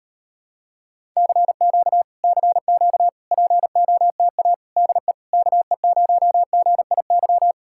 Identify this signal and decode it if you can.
Morse code, heard as a keyed tone, about 38 wpm, 700 Hz. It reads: CQ CQ POTA DE KE0GIY